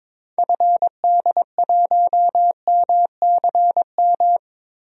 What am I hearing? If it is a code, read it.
Morse code, heard as a keyed tone, about 22 wpm, 700 Hz. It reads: FD1MCM
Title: DXCC-100 Call Signs - 6 Characters - Repeated 3x - 22wpm